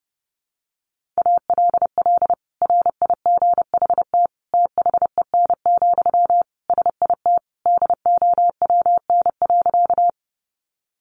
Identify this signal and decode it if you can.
Morse code, heard as a keyed tone, about 30 wpm, 700 Hz. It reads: ALL RIGHT THEN, SIT DOWN.
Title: Sentences from Top 500 Words 30wpm